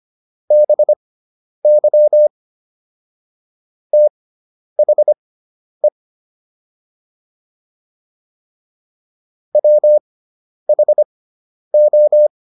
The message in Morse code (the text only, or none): BY THE WHO